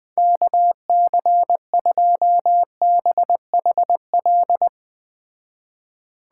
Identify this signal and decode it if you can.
Morse code, heard as a keyed tone, about 20 wpm, 700 Hz. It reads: KC2BHL
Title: Speed Racing - USA 2x3 Call Signs 17wpm